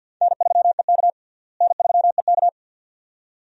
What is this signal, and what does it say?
Morse code, heard as a keyed tone, about 50 wpm, 700 Hz. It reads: N3EK N3EK
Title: USA Call Signs - 4 Characters - Repeated 2x - 50wpm